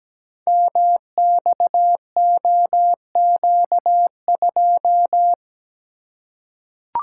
Morse code, heard as a keyed tone, about 17 wpm, 700 Hz. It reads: MXOQ2
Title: Code Groups - 5 Letter:Numbers - 17wpm